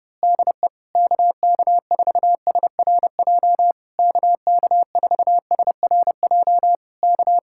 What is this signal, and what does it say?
Morse code, heard as a keyed tone, about 29 wpm, 700 Hz. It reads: DE KK4SRJ KK4SRJ K
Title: Speed-Racing Common QSO Phrases Encore - 25wpm